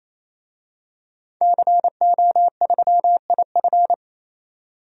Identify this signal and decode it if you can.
Morse code, heard as a keyed tone, about 28 wpm, 700 Hz. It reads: CO3IF
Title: DXCC-100 Call Signs - 5 Characters - 28wpm